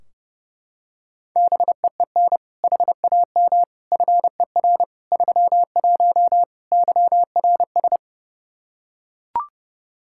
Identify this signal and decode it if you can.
Morse code, heard as a keyed tone, about 30 wpm, 700 Hz. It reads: BEEN HAM FER 31 YRS
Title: Practice QSO - Exchange 3 - More Info - 30wpm